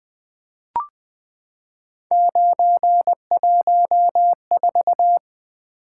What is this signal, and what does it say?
Morse code, heard as a keyed tone, about 20 wpm, 700 Hz. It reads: 914
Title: Three Letter-Number Combinations - 20wpm